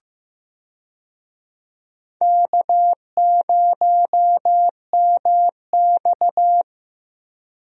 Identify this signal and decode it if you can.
Morse code, heard as a keyed tone, about 16 wpm, 700 Hz. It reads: K0MX
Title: USA Call Signs - 4 Characters - 15wpm